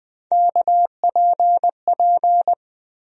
Morse code, heard as a keyed tone, about 20 wpm, 700 Hz. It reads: KPP
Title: USA Call Sign Postfixes - 3 Letters 20wpm